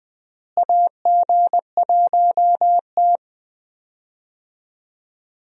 Morse code, heard as a keyed tone, about 20 wpm, 700 Hz. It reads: AG1T
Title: USA Call Signs - 4 Characters - Repeated 3x - 20wpm